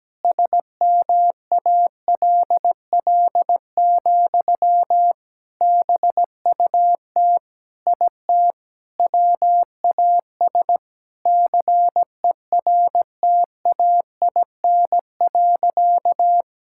Morse code, heard as a keyed tone, about 17 wpm, 700 Hz. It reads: SMALL, BUT IT WAS CERTAIN.
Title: Sentences from Top 1000 Words - Review Part 6 - 17wpm